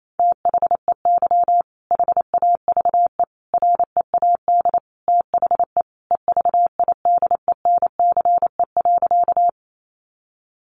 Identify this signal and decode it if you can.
Morse code, heard as a keyed tone, about 28 wpm, 700 Hz. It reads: THEY HAVE READ THE EVIDENCE.
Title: Sentences from Top 1300 Words 28wpm